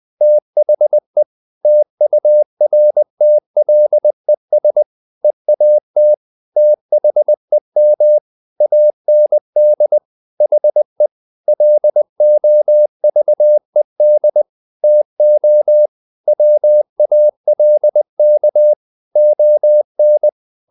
Morse code, heard as a keyed tone, about 20 wpm, 600 Hz. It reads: THE TURTLES EAT THEM AND HE LOVED TO WALK ON
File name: the old man and the sea / the old man and the sea part 1 20wpm